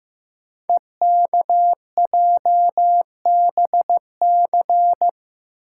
Morse code, heard as a keyed tone, about 15 wpm, 700 Hz. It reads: EKJBC
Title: Code Groups - 5 Letters - 15wpm